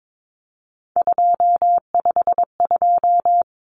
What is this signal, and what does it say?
Morse code, heard as a keyed tone, about 22 wpm, 700 Hz. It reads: 252
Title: Common Cut Numbers - 3-Digit 22wpm